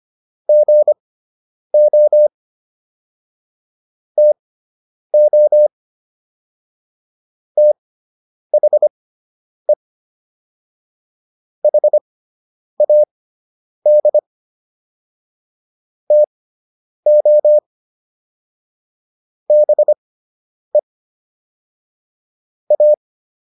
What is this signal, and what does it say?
Morse code, heard as a keyed tone, about 25 wpm, spaced as for 9 wpm, 600 Hz. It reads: GO TO THE HAD TO BE A